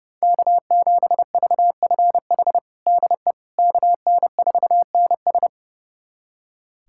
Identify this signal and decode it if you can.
Morse code, heard as a keyed tone, about 32 wpm, 700 Hz. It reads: K7VFH DE KN4NS
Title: USA Callsign DE Callsign - 5 Characters - 30wpm